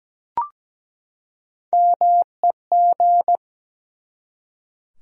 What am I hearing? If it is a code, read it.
Morse code, heard as a keyed tone, about 17 wpm, 700 Hz. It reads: MEG
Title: USA Common Names - Max 5 Letters - 17wpm